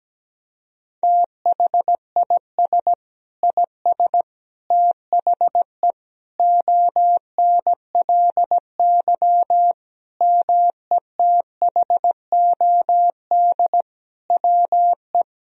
Morse code, heard as a keyed tone, about 17 wpm, 700 Hz. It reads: THIS IS THE ONLY METHOD WE
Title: Sentences from Top 1000 Words - Review Part 3 - 17wpm